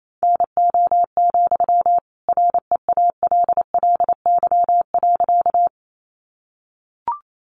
Morse code, heard as a keyed tone, about 28 wpm, 700 Hz. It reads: NO, REALLY.
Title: Sentences from Top 600 Words 28wpm